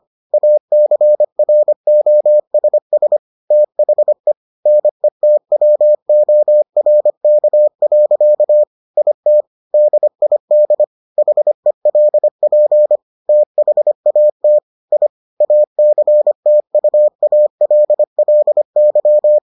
Morse code, heard as a keyed tone, about 25 wpm, 600 Hz. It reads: ACROSS THE NETWORK. IT DID HELP THAT I ACTUALLY